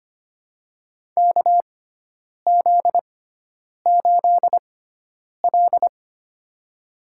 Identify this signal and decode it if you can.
Morse code, heard as a keyed tone, about 25 wpm, spaced as for 9 wpm, 700 Hz. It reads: KZ8L
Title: USA Call Signs - 4 Characters - 6x Character Spacing - 25wpm